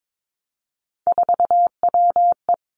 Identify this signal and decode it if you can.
Morse code, heard as a keyed tone, about 22 wpm, 700 Hz. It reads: 4WE
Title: Three Challenging Letter-Number Combinations 22wpm